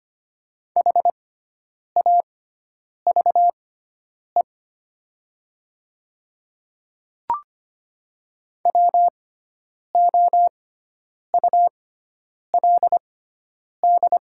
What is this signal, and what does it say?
Morse code, heard as a keyed tone, about 25 wpm, spaced as for 9 wpm, 700 Hz. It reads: HAVE WOULD
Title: 6x Character Spacing - Top 100 Words - 25wpm